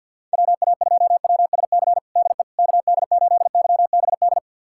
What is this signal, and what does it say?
Morse code, heard as a keyed tone, about 51 wpm, 690 Hz. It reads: WA2WSC DE KD8YBD
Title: USA Callsign DE Callsign - 6 Characters - 50wpm